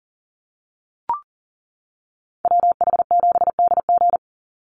Morse code, heard as a keyed tone, about 40 wpm, 700 Hz. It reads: WH7DG
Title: USA Call Signs 40wpm